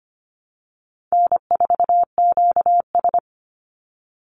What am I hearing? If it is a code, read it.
Morse code, heard as a keyed tone, about 25 wpm, 700 Hz. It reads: N4QS
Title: Easy Call Signs - 25wpm